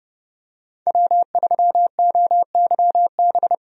Morse code, heard as a keyed tone, about 30 wpm, 700 Hz. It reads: W3OYB
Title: USA Call Signs - 5 Characters - 30wpm